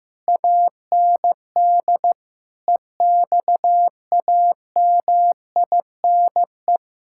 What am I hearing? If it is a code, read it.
Morse code, heard as a keyed tone, about 15 wpm, 700 Hz. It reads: AND EXAMINE
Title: Sentences from Top 1600 Words 15wpm